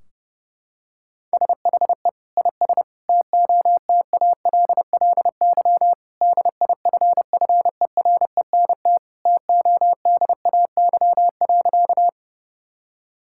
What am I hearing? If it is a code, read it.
Morse code, heard as a keyed tone, about 32 wpm, 700 Hz. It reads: SHE IS TOTALLY DIFFERENT TODAY.
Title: Sentences from Top 1600 Words 30wpm